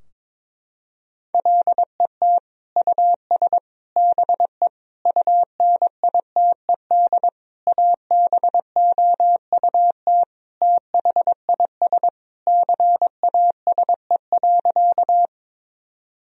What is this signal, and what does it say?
Morse code, heard as a keyed tone, about 22 wpm, 700 Hz. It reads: LET US BE UNITED ABOUT THIS CASE.
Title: Sentences from Top 1000 Words - Review Part 4 - 22wpm